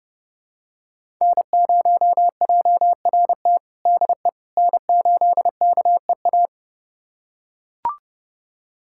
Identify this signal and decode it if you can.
Morse code, heard as a keyed tone, about 32 wpm, 700 Hz. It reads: N0JRT DE N8KEA
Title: USA Callsign DE Callsign - 5 Characters - 30wpm